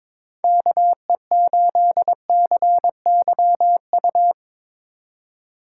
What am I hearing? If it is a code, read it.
Morse code, heard as a keyed tone, about 24 wpm, 700 Hz. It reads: KE8CYU
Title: Speed Racing - USA Call Signs 22wpm